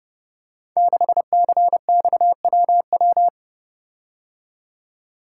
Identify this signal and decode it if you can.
Morse code, heard as a keyed tone, about 30 wpm, 700 Hz. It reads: 6CXWW
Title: Code Groups - 5 Letter:Numbers - 30wpm